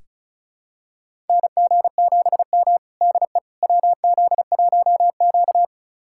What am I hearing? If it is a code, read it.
Morse code, heard as a keyed tone, about 36 wpm, 700 Hz. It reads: NG7M DE WZ1Q
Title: USA Callsign DE Callsign - 4 Characters - 35wpm